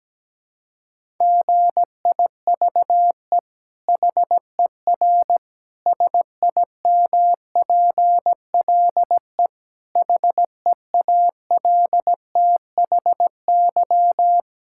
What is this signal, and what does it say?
Morse code, heard as a keyed tone, about 17 wpm, 700 Hz. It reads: GIVE HER SIMPLE HEALTHY
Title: Sentences from Top 1000 Words - Review Part 5 - 17wpm